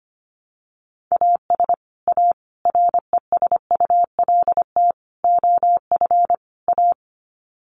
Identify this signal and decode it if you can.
Morse code, heard as a keyed tone, about 25 wpm, 700 Hz. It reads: AS A RESULT OF A
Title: Sets of 5 Words Encore 25wpm